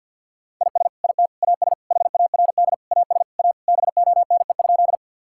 Morse code, heard as a keyed tone, about 50 wpm, 700 Hz. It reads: IS IT AS HARD AS A BONE?
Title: Sentences from Top 1600 Words 50wpm